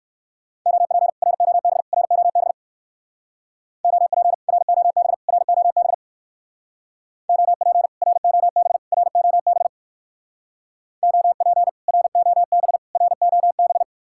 Morse code, heard as a keyed tone, about 59 wpm, 700 Hz. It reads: OP ROB ROB OP ROB ROB OP ROB ROB OP ROB ROB